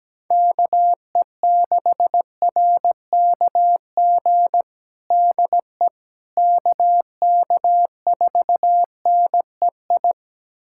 Morse code, heard as a keyed tone, about 17 wpm, 700 Hz. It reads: KE6RKG DE KK4NEI
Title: USA Callsign DE Callsign 17wpm